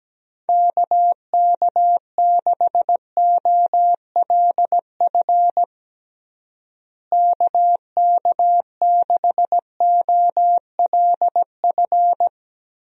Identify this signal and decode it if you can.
Morse code, heard as a keyed tone, about 17 wpm, 700 Hz. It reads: KK6OLF KK6OLF
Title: USA Call Signs - 6 Characters - Repeated 3x - 17wpm